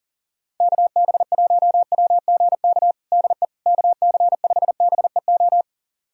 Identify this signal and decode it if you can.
Morse code, heard as a keyed tone, about 40 wpm, 700 Hz. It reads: KB1WGK DE KC5BEO
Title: USA Callsign DE Callsign - 6 Characters - 40wpm